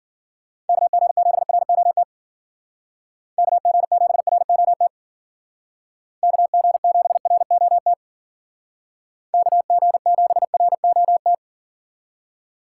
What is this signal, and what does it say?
Morse code, heard as a keyed tone, about 52 wpm, 700 Hz. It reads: KG7ROT KG7ROT KG7ROT KG7ROT